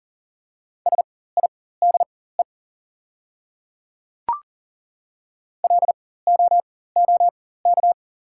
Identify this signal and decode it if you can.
Morse code, heard as a keyed tone, about 40 wpm, spaced as for 19 wpm, 690 Hz. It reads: SIDE LOOK
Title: 4x Character Spacing - Top 100 Words - 40wpm